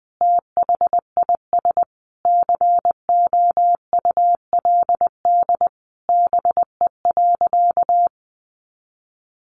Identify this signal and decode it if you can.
Morse code, heard as a keyed tone, about 20 wpm, 700 Hz. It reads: THIS COULD BE.
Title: Sentences from Top 1400 Words 20wpm